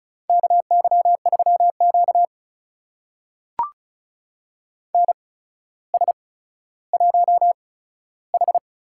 Morse code, heard as a keyed tone, about 36 wpm, spaced as for 15 wpm, 700 Hz. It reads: KY3Q NS1H
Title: USA Call Signs - 4 Characters - 8x Character Spacing - 35wpm